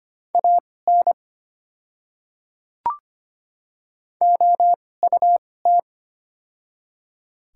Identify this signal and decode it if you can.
Morse code, heard as a keyed tone, about 25 wpm, spaced as for 18 wpm, 690 Hz. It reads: AN OUT
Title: 2x Character Spacing - Top 100 Words - 25wpm